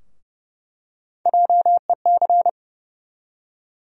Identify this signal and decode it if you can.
Morse code, heard as a keyed tone, about 30 wpm, 700 Hz. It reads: JEC
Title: USA Call Sign Postfixes - 3 Letters 30wpm